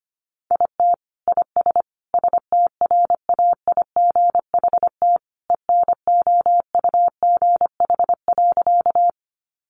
Morse code, heard as a keyed tone, about 25 wpm, 700 Hz. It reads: IT IS STRAIGHT ENOUGH.
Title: Sentences from Top 1000 Words 25wpm